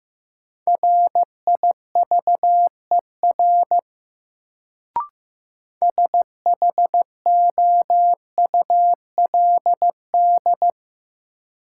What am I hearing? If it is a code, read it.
Morse code, heard as a keyed tone, about 15 wpm, 700 Hz. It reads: RIVER SHOULD
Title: Sentences from Top 300 Words - New Word Drill - 15wpm